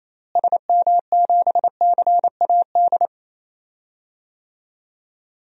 Morse code, heard as a keyed tone, about 28 wpm, 700 Hz. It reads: SM7CAD
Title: DXCC-100 Call Signs - 6 Characters - Repeated 2x - 28wpm